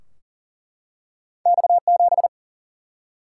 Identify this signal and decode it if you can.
Morse code, heard as a keyed tone, about 40 wpm, 700 Hz. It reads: X7
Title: Single-Letter Single-Number 40wpm